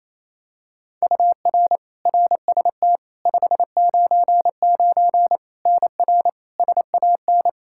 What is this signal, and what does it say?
Morse code, heard as a keyed tone, about 28 wpm, 700 Hz. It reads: UR RST 599 NR SAN
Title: Practice QSO - Exchange 1 - Beginning - 28wpm